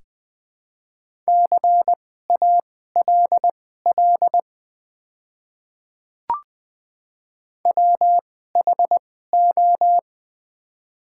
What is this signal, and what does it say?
Morse code, heard as a keyed tone, about 20 wpm, spaced as for 15 wpm, 690 Hz. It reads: CALL WHO
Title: 2x Character Spacing - Top 500 Words - 20wpm